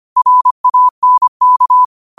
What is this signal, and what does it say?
Morse code, heard as a keyed tone, about 25 wpm, 1000 Hz. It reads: RANK
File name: qotdfiles / 25wpm